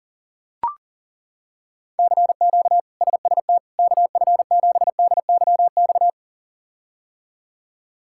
Morse code, heard as a keyed tone, about 37 wpm, 700 Hz. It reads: CQ SST KF7DYX
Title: K1USN SST QSO Practice - 40wpm